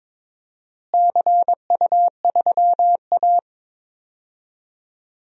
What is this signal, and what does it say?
Morse code, heard as a keyed tone, about 22 wpm, 700 Hz. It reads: CU3A
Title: DXCC-100 Call Signs - 4 Characters - Repeated 2x - 22wpm